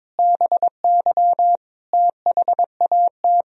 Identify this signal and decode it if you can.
Morse code, heard as a keyed tone, about 22 wpm, 700 Hz. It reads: BY THAT
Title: Sentences from Top 400 Words and Abbreviations 22wpm